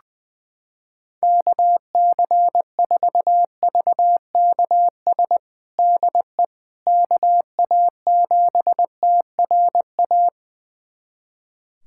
Morse code, heard as a keyed tone, about 20 wpm, 700 Hz. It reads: KC4VKS DE KA7TRA
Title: USA Callsign DE Callsign 20wpm